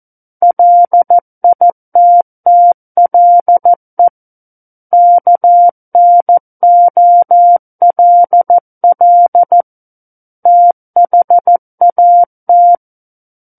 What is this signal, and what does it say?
Morse code, heard as a keyed tone, about 14 wpm, 700 Hz. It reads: LITTLE KNOLL THAT
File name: Book One / 05-Chapter 5 - 14 WPM